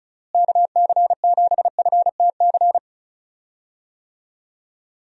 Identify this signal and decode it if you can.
Morse code, heard as a keyed tone, about 34 wpm, 700 Hz. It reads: KC7FTC
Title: USA Call Signs - 6 Characters - Repeated 3x - 35wpm